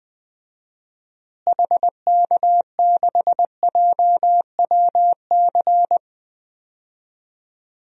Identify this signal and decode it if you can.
Morse code, heard as a keyed tone, about 20 wpm, 700 Hz. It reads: HK6JWC
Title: DXCC-100 Call Signs - 6 Characters - Repeated 3x - 20wpm